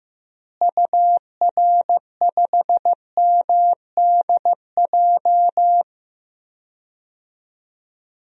Morse code, heard as a keyed tone, about 15 wpm, 700 Hz. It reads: UR5MDJ
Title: DXCC-100 Call Signs - 6 Characters - Repeated 3x - 15wpm